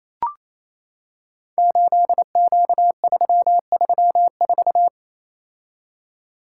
Morse code, heard as a keyed tone, about 28 wpm, 700 Hz. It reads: 8Q334
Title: Code Groups - 5 Letter:Numbers - 28wpm